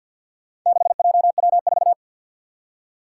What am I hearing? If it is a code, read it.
Morse code, heard as a keyed tone, about 52 wpm, 700 Hz. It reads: 6JW4
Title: Four Challenging Letter-Number Combinations 50wpm